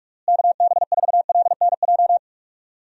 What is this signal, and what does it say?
Morse code, heard as a keyed tone, about 46 wpm, 700 Hz. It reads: KB4LNJ
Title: USA Call Signs - 6 Characters - Repeated 3x - 45wpm